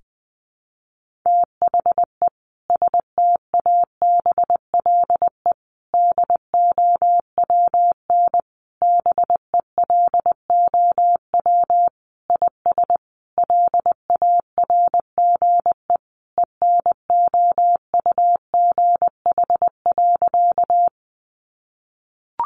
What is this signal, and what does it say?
Morse code, heard as a keyed tone, about 19 wpm, 700 Hz. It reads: THE STABLE DOWN BELOW IS LARGE ENOUGH.
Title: Sentences from Top 1500 Words 20wpm